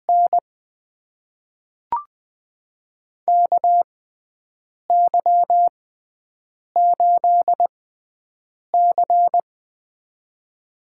Morse code, heard as a keyed tone, about 20 wpm, spaced as for 7 wpm, 700 Hz. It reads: N KY8C
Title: USA Call Signs - 4 Characters - 6x Character Spacing - 20wpm